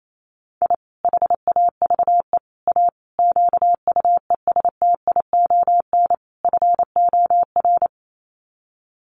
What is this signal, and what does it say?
Morse code, heard as a keyed tone, about 28 wpm, 700 Hz. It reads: I HAVE A QUESTION FOR
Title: Sets of 5 Words Encore2 28wpm